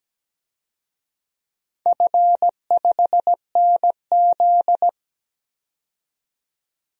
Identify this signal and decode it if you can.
Morse code, heard as a keyed tone, about 17 wpm, 700 Hz. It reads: F5NZ
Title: DXCC-100 Call Signs - 4 Characters - 17wpm